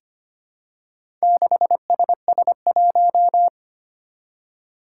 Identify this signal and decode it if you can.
Morse code, heard as a keyed tone, about 25 wpm, 700 Hz. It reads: 6SS1
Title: Four Challenging Letter-Number Combinations 25wpm